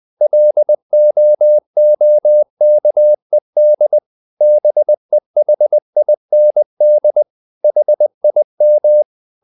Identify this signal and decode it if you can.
Morse code, heard as a keyed tone, about 20 wpm, 600 Hz. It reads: LOOKED BEHIND HIM
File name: the old man and the sea / the old man and the sea part 1 20wpm